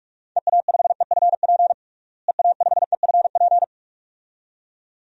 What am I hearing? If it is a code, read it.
Morse code, heard as a keyed tone, about 46 wpm, 700 Hz. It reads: EA5EFP EA5EFP
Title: DXCC-100 Call Signs - 6 Characters - Repeated 2x - 45wpm